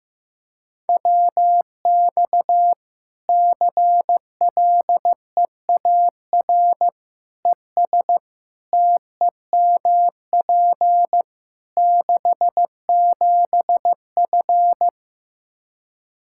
Morse code, heard as a keyed tone, about 15 wpm, 700 Hz. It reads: WX CLEAR ES TEMP 67F
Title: K1USN SST QSO Practice 15wpm